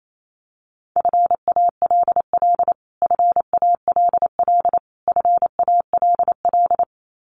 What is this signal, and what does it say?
Morse code, heard as a keyed tone, about 28 wpm, 700 Hz. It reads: FALL FALL FALL
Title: Top 500 Words Repeated 28wpm